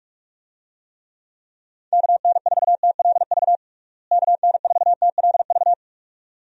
Morse code, heard as a keyed tone, about 43 wpm, 700 Hz. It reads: KN4TLV KN4TLV
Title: USA Call Signs - 6 Characters - Repeated 2x - 45wpm